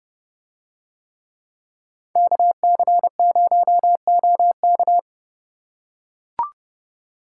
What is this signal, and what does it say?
Morse code, heard as a keyed tone, about 29 wpm, 700 Hz. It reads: KC0OK